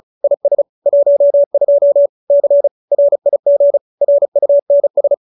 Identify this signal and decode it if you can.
Morse code, heard as a keyed tone, about 34 wpm, 570 Hz. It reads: IS 12 C RIG RUNS
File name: QSO304 35